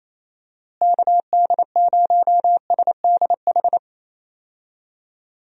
Morse code, heard as a keyed tone, about 28 wpm, 700 Hz. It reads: KD0SDH
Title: USA Call Signs - 6 Characters - Repeated 3x - 28wpm